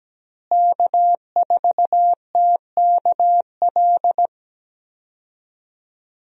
Morse code, heard as a keyed tone, about 17 wpm, 700 Hz. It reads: K4TKL
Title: USA Call Signs - 5 Characters - Repeated 2x - 17wpm